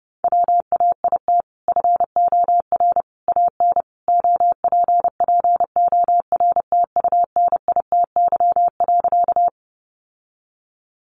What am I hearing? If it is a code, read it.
Morse code, heard as a keyed tone, about 30 wpm, 700 Hz. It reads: WAIT FOR AN OPPORTUNITY.
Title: Sentences from Top 1000 Words 30wpm